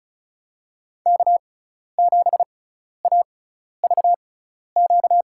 Morse code, heard as a keyed tone, about 36 wpm, spaced as for 12 wpm, 700 Hz. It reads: K7AVQ
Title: USA Call Signs - 5 Characters - 6x Character Spacing - 35wpm